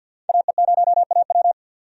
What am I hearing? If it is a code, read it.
Morse code, heard as a keyed tone, about 52 wpm, 700 Hz. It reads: AE0AW
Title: USA Call Signs - 5 Characters - 50wpm